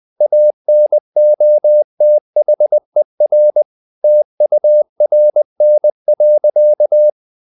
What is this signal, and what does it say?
Morse code, heard as a keyed tone, about 20 wpm, 600 Hz. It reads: ANOTHER TURN.
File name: the old man and the sea / the old man and the sea part 1 20wpm